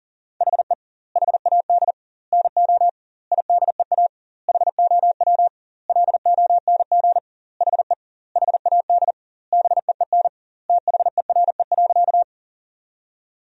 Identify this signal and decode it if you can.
Morse code, heard as a keyed tone, about 37 wpm, 700 Hz. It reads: HE HAD NO IDEA HOW LONG HE HAD BEEN THERE.
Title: 2x Word-Spacing - Sentences from Top 300 Words - 40wpm